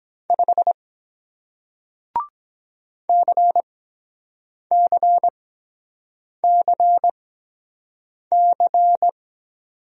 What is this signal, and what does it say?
Morse code, heard as a keyed tone, about 23 wpm, spaced as for 7 wpm, 700 Hz. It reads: 5 CCCC